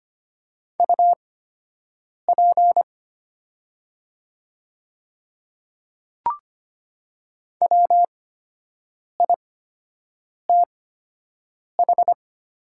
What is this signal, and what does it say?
Morse code, heard as a keyed tone, about 25 wpm, spaced as for 7 wpm, 700 Hz. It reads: UP WITH
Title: 8x Character Spacing - Top 100 Words - 25wpm